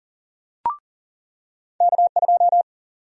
Morse code, heard as a keyed tone, about 40 wpm, 700 Hz. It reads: K2
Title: USA Call Sign Prefixes - 40wpm